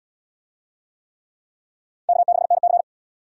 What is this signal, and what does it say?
Morse code, heard as a keyed tone, about 74 wpm, 700 Hz. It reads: K6NC